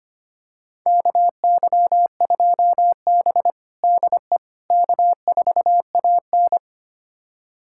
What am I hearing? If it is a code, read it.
Morse code, heard as a keyed tone, about 25 wpm, 700 Hz. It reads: KY2B DE K4AN
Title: USA Callsign DE Callsign - 4 Characters - 25wpm